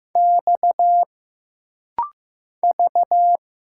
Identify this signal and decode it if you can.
Morse code, heard as a keyed tone, about 15 wpm, 700 Hz. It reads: X V
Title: Single Letters - Rapid-Fire 15wpm